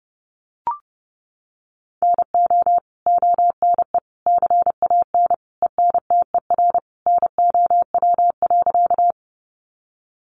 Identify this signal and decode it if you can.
Morse code, heard as a keyed tone, about 30 wpm, 700 Hz. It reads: NO ONE CAN ENTER NOW.
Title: Sentences from Top 1300 Words 30wpm